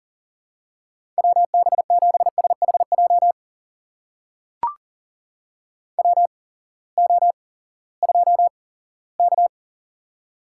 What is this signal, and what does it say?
Morse code, heard as a keyed tone, about 38 wpm, spaced as for 20 wpm, 700 Hz. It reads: WB7SHJ WO2K
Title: USA Call Signs - 8x Character Spacing - 40wpm